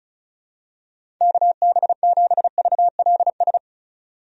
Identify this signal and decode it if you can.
Morse code, heard as a keyed tone, about 36 wpm, 700 Hz. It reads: KB7VLS
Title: USA Call Signs - 6 Characters - 35wpm